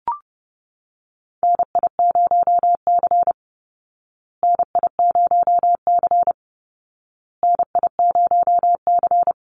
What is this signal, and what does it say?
Morse code, heard as a keyed tone, about 30 wpm, 700 Hz. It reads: NI0C NI0C NI0C
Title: Call Signs Repeated 2x - 30wpm